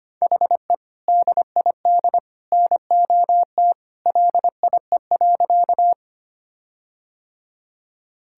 Morse code, heard as a keyed tone, about 25 wpm, 700 Hz. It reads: HE DID NOT LIE.
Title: Sentences from Top 1600 Words 25wpm